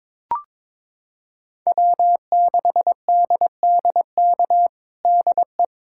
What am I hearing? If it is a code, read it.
Morse code, heard as a keyed tone, about 22 wpm, 700 Hz. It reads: W6DDK DE
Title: USA Callsign DE Callsign - 5 Characters - 22wpm